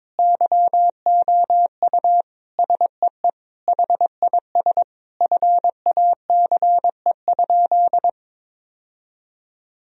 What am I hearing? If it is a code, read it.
Morse code, heard as a keyed tone, about 22 wpm, 700 Hz. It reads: YOU SEE HIS FACE?
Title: Sentences from Top 1000 Words - Review Part 2 - 22wpm